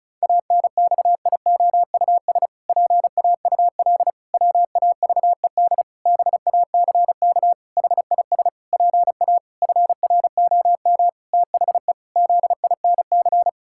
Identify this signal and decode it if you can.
Morse code, heard as a keyed tone, about 34 wpm, 700 Hz. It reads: ANXIOUS PAUL WAVED BACK HIS PA FROM THE ZINC
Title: Pangram Challenge v2 - 35wpm